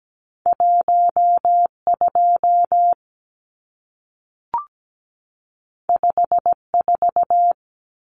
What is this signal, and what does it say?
Morse code, heard as a keyed tone, about 17 wpm, 700 Hz. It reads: 12 54
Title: Numbers - 2-Digit 17wpm